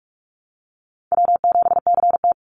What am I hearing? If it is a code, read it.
Morse code, heard as a keyed tone, about 44 wpm, 700 Hz. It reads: W7CT
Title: Easy Call Signs - 45wpm